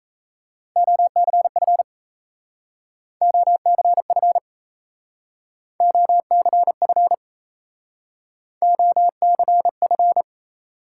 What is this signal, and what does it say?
Morse code, heard as a keyed tone, about 34 wpm, 700 Hz. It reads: OCF OCF OCF OCF
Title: Speed-Racing Top 180 QSO Elements 28wpm